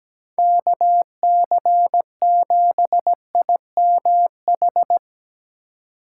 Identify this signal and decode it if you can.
Morse code, heard as a keyed tone, about 19 wpm, 700 Hz. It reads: KC7IMH
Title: Speed Racing - USA Call Signs 17wpm